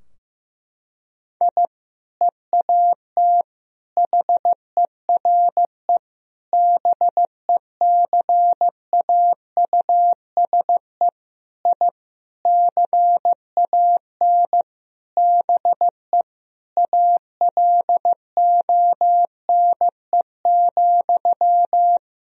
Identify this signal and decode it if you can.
Morse code, heard as a keyed tone, about 16 wpm, 700 Hz. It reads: I EAT HERE BECAUSE I CAN BE ALONE,
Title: Sentences from Top 1000 Words - Review Part 3 - 15wpm